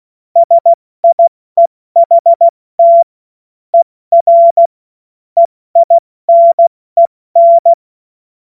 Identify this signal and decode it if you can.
Morse code, heard as a keyed tone, about 16 wpm, spaced as for 14 wpm, 680 Hz. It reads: SIEHT ER EINEN
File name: Morsekurs / Witze / Witz 31. Mai 2022 70 F80